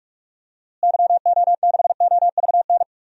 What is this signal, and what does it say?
Morse code, heard as a keyed tone, about 46 wpm, 700 Hz. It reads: YO6OVN
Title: DXCC-100 Call Signs - 6 Characters - Repeated 2x - 45wpm